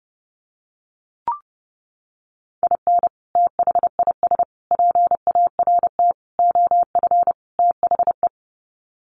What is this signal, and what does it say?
Morse code, heard as a keyed tone, about 30 wpm, 700 Hz. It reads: IN THIS PART OF THE
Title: Sets of 5 Words 30wpm